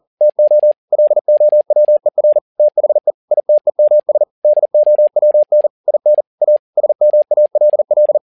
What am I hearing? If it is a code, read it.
Morse code, heard as a keyed tone, about 40 wpm, 600 Hz. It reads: TO LOWER THE ITEMS DOWN IN A SMALL
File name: learning center / tower trouble / tower touble 40wpm